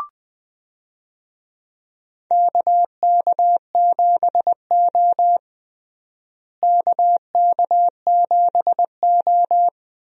Morse code, heard as a keyed tone, about 20 wpm, 700 Hz. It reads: KK7O KK7O
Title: USA Call Signs - 4 Characters - Repeated 3x - 20wpm